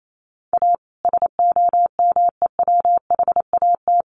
Morse code, heard as a keyed tone, about 28 wpm, 700 Hz. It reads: A SOMEWHAT
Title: Sentences from Top 1300 Words 28wpm